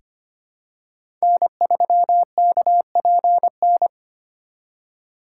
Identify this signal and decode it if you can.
Morse code, heard as a keyed tone, about 25 wpm, 700 Hz. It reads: N3KPN
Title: USA Call Signs - 5 Characters - 4x Character Spacing - 25wpm